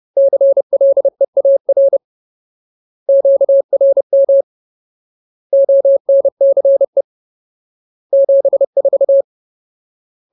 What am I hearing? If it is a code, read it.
Morse code, heard as a keyed tone, about 30 wpm, 550 Hz. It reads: CLEAR QRM ONCE 74
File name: Ultimate Word List / 30WPM Ultimate